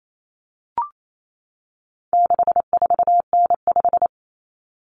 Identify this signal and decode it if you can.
Morse code, heard as a keyed tone, about 28 wpm, 700 Hz. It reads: 64N5
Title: Common Cut Numbers - 4-Digit 28wpm